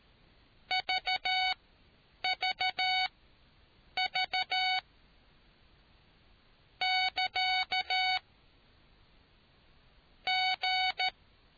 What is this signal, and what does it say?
Morse code, heard as a keyed tone, about 13 wpm, spaced as for 8 wpm, 750 Hz. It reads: VVV <KA> G